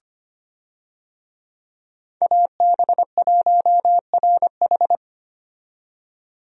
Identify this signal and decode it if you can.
Morse code, heard as a keyed tone, about 25 wpm, 700 Hz. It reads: AB1RH
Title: USA Call Signs - 5 Characters - 25wpm